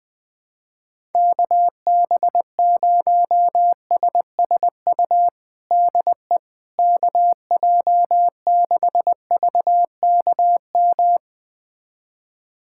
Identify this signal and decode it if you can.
Morse code, heard as a keyed tone, about 20 wpm, 700 Hz. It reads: KB0SSU DE KJ6VKM
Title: USA Callsign DE Callsign - 6 Characters - 20wpm